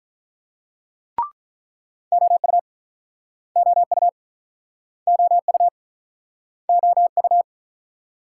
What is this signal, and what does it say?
Morse code, heard as a keyed tone, about 47 wpm, 700 Hz. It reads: OU OU OU OU